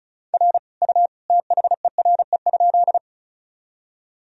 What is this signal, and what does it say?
Morse code, heard as a keyed tone, about 34 wpm, 700 Hz. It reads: R U THERE?
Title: Sentences from Top 100 Words and Abbreviations 35wpm